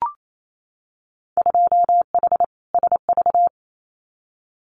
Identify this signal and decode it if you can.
Morse code, heard as a keyed tone, about 28 wpm, 700 Hz. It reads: 2H SV
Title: Winter Field Day Exchanges Encore 28wpm